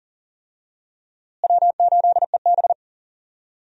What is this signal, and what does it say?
Morse code, heard as a keyed tone, about 39 wpm, 700 Hz. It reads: W8EB
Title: USA Call Signs - 4 Characters - Repeated 3x - 40wpm